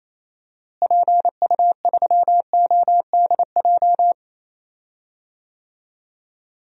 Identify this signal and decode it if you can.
Morse code, heard as a keyed tone, about 27 wpm, 700 Hz. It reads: PU3ODJ
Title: DXCC-100 Call Signs - 6 Characters - Repeated 3x - 28wpm